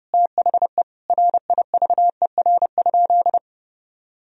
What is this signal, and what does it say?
Morse code, heard as a keyed tone, about 30 wpm, 700 Hz. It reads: THE RIVER?
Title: Sentences from Top 1000 Words - Review Part 4 - 30wpm